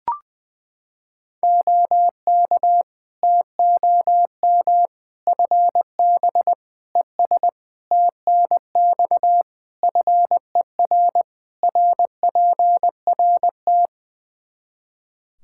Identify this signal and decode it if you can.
Morse code, heard as a keyed tone, about 20 wpm, 700 Hz. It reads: OK TOM FB ES TNX FER RPRT
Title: Practice QSO - Exchange 2 - Info - 20wpm